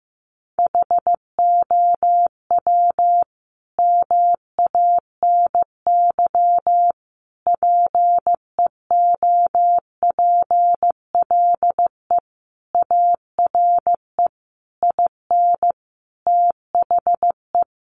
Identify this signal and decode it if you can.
Morse code, heard as a keyed tone, about 15 wpm, 700 Hz. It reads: HOW MANY PEOPLE ARE IN THE
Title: Sentences from Top 1200 Words 15wpm